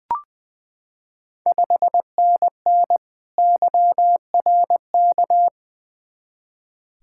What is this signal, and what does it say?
Morse code, heard as a keyed tone, about 20 wpm, 700 Hz. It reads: 5NN YRK
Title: Ontario QSO Party Exchange 20wpm